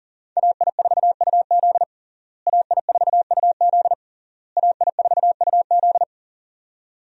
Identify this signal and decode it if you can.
Morse code, heard as a keyed tone, about 40 wpm, 700 Hz. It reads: AI4UZ AI4UZ AI4UZ
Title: USA Call Signs - 5 Characters - Repeated 3x - 40wpm